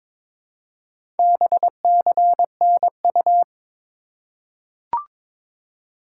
Morse code, heard as a keyed tone, about 22 wpm, 700 Hz. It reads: BCNU
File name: Top 90 QSO Elements - 4x Character Spacing - 22wpm